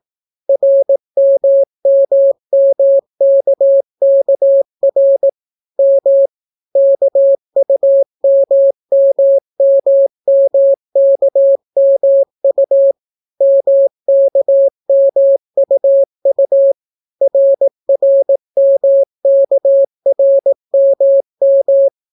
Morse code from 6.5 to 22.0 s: KUMMMMKMU MKMUU RRMKRMM